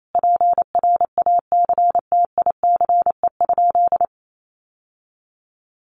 Morse code, heard as a keyed tone, about 28 wpm, 700 Hz. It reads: PRACTICE?
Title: Sentences from Top 700 Words 28wpm